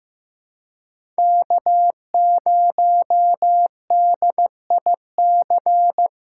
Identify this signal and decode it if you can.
Morse code, heard as a keyed tone, about 15 wpm, 700 Hz. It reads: K0DIC
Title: USA Callsign DE Callsign - 5 Characters - 15wpm